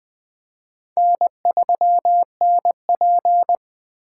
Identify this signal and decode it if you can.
Morse code, heard as a keyed tone, about 20 wpm, 700 Hz. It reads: N3NP
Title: USA Call Signs - 4 Characters - 4x Character Spacing - 20wpm